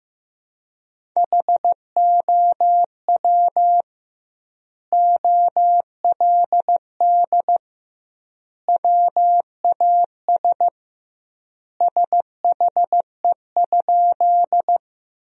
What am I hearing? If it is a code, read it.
Morse code, heard as a keyed tone, about 15 wpm, 700 Hz. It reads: HOW OLD WAS SHE?
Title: 2x Word-Spacing - Sentences from Top 200 Words - 15wpm